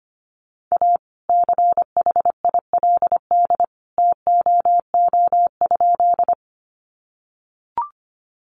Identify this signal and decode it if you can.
Morse code, heard as a keyed tone, about 25 wpm, 700 Hz. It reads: A CHILD TOO?
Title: Sentences from Top 800 Words 25wpm